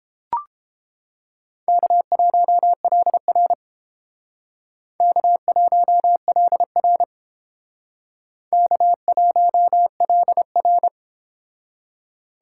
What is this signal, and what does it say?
Morse code, heard as a keyed tone, about 30 wpm, 700 Hz. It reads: K1LR K1LR K1LR